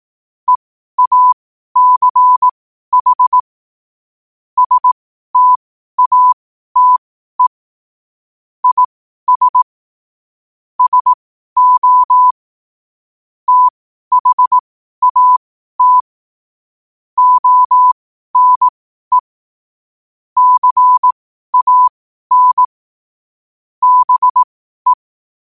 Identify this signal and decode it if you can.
Morse code, heard as a keyed tone, about 18 wpm, spaced as for 12 wpm, 1000 Hz. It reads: EACH STATE IS SO THAT ONE CAN BE
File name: qotdfiles / 12wpm